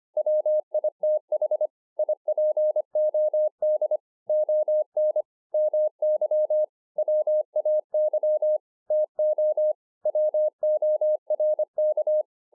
Morse code, heard as a keyed tone, about 25 wpm, 610 Hz. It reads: WITH IPOD ON MY WAY TO WORK
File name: 2011-04-mp3 / THIS IS NOT MY SHACK 2 25WPM